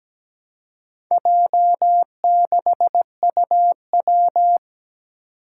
Morse code, heard as a keyed tone, about 17 wpm, 700 Hz. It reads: J6UW
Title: Four Challenging Letter-Number Combinations 17wpm